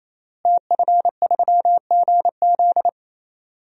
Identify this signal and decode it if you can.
Morse code, heard as a keyed tone, about 28 wpm, 700 Hz. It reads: TF3GZ
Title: DXCC-100 Call Signs - 5 Characters - Repeated 3x - 28wpm